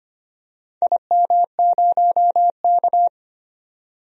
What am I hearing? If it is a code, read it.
Morse code, heard as a keyed tone, about 25 wpm, 700 Hz. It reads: IM0K
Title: DXCC-100 Call Signs - 4 Characters - 25wpm